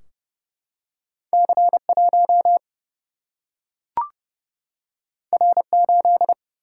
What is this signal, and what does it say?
Morse code, heard as a keyed tone, about 29 wpm, 700 Hz. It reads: C 1 R 8
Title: Single-Letter Single-Number 30wpm